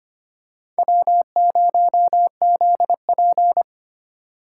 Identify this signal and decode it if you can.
Morse code, heard as a keyed tone, about 25 wpm, 700 Hz. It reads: W0ZP
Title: K1USN SST QSO Practice - 25wpm